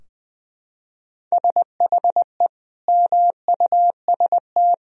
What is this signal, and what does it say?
Morse code, heard as a keyed tone, about 21 wpm, 700 Hz. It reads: SHE MUST
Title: Sentences from Top 1000 Words - Review Part 2 - 20wpm